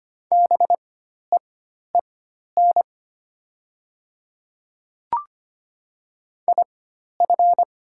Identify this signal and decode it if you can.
Morse code, heard as a keyed tone, about 25 wpm, spaced as for 12 wpm, 700 Hz. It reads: BEEN IF
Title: 4x Character Spacing - Top 100 Words - 25wpm